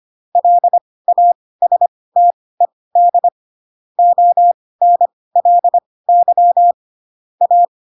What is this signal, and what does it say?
Morse code, heard as a keyed tone, about 25 wpm, spaced as for 18 wpm, 700 Hz. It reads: LASTED ONLY A
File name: Pinocchio / Pinocchio ch4 12 25 ik1hge